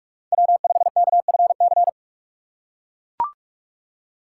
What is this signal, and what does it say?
Morse code, heard as a keyed tone, about 44 wpm, 700 Hz. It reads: W5KFC